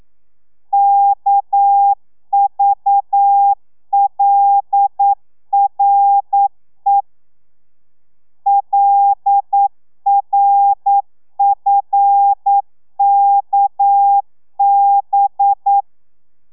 Morse code, heard as a keyed tone, about 9 wpm, 800 Hz. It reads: KVLRE LRFKB
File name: telegrafia / i2tfj / Lez6 40